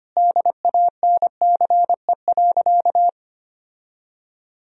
Morse code, heard as a keyed tone, about 25 wpm, 700 Hz. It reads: DANCE.
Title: Sentences from Top 1000 Words - Review Part 4 - 25wpm